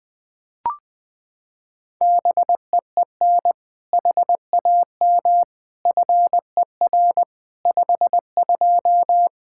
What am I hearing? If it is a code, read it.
Morse code, heard as a keyed tone, about 20 wpm, 700 Hz. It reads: BEEN HAM FER 52
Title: Practice QSO - Exchange 3 - More Info - 20wpm